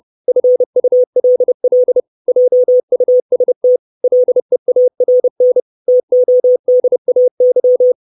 Morse code, heard as a keyed tone, about 30 wpm, 500 Hz. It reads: FULL JUST LEARN TODAY
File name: 632 Words 30WPM